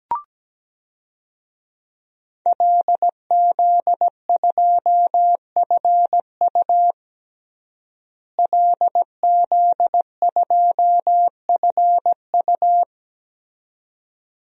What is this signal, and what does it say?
Morse code, heard as a keyed tone, about 17 wpm, 700 Hz. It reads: LZ2FU LZ2FU
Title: DXCC-100 Call Signs - 5 Characters - Repeated 2x - 17wpm